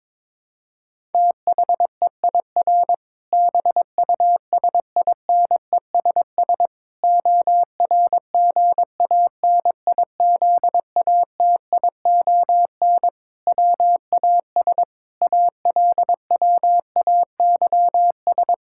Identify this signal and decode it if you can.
Morse code, heard as a keyed tone, about 22 wpm, 700 Hz. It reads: THEIR BUSINESS ORGANIZATION WAS ALWAYS
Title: Sentences from Top 1000 Words - Review Part 5 - 22wpm